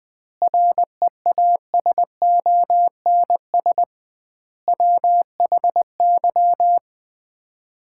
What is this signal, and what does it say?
Morse code, heard as a keyed tone, about 20 wpm, 700 Hz. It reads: REASONS WHY
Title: 2x Word-Spacing - Sets of 5 Words - 20wpm